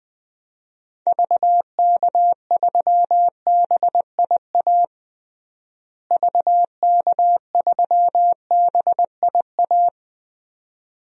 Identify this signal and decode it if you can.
Morse code, heard as a keyed tone, about 20 wpm, 700 Hz. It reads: VK3BIA VK3BIA
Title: DXCC-100 Call Signs - 6 Characters - Repeated 2x - 20wpm